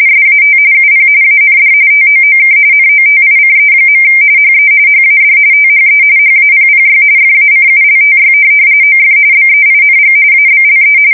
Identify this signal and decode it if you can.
RTTY, about 45.45 baud. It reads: SAT UP AND TOOK NOTICE. HE HAD THE HAPPY FACULTY OF BEING ABLE